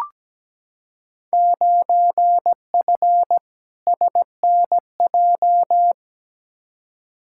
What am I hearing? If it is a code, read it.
Morse code, heard as a keyed tone, about 17 wpm, 700 Hz. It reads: 9F SNJ
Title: Field Day Exchanges Encore - 17wpm